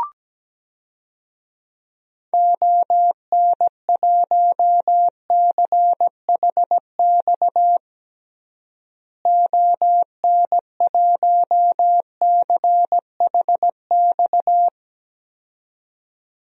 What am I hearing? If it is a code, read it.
Morse code, heard as a keyed tone, about 17 wpm, 700 Hz. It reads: ON1CHX ON1CHX
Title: DXCC-100 Call Signs - 6 Characters - Repeated 2x - 17wpm